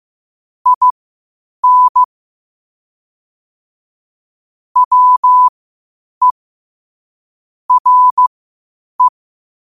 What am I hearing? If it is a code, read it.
Morse code, heard as a keyed tone, about 15 wpm, spaced as for 8 wpm, 1000 Hz. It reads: IN WE RE
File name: qotdfiles / 8wpm